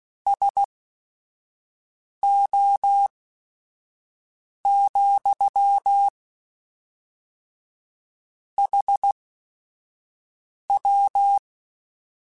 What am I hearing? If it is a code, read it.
Morse code, heard as a keyed tone, about 16 wpm, spaced as for 5 wpm, 800 Hz. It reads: SO, HW
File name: QSOs / QSOs-mp3s / WD0BGP 1@5